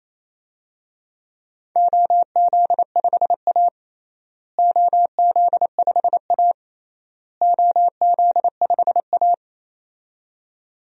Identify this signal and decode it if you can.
Morse code, heard as a keyed tone, about 28 wpm, 700 Hz. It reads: OZ5A OZ5A OZ5A
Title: DXCC-100 Call Signs - 4 Characters - Repeated 3x - 28wpm